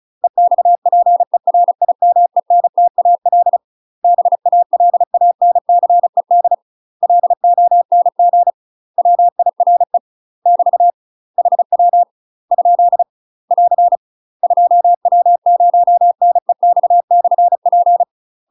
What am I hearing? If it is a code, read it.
Morse code, heard as a keyed tone, about 36 wpm, 700 Hz. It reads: EXPERIMENTAL BALANCED LONG WIRE <BT> HW ? <AR> 2W0NEX/P
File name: UK MASTER QSOs TRAINING / 35WPM / UK MASTER QSOs 0032